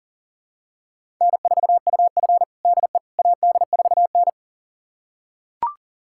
Morse code, heard as a keyed tone, about 40 wpm, 700 Hz. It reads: N4UF DE AD4N
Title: USA Callsign DE Callsign - 4 Characters - 40wpm